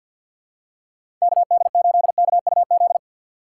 Morse code, heard as a keyed tone, about 51 wpm, 700 Hz. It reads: K D 8 K U Z
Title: USA Call Signs - 6 Characters - 6x Character Spacing - 50wpm